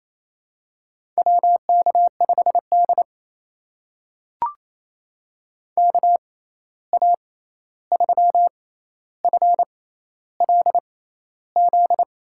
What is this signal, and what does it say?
Morse code, heard as a keyed tone, about 28 wpm, spaced as for 13 wpm, 700 Hz. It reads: WK5D KA3FLZ
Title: USA Call Signs - 6x Character Spacing - 28wpm